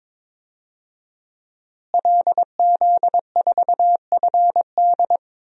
Morse code, heard as a keyed tone, about 21 wpm, 700 Hz. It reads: LZ4FD
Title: DXCC-100 Call Signs - 5 Characters - Repeated 3x - 22wpm